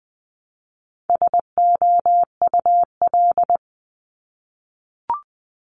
Morse code, heard as a keyed tone, about 20 wpm, 700 Hz. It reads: SOUL
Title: Multi-letter Phonemes Repeated 3x 20wpm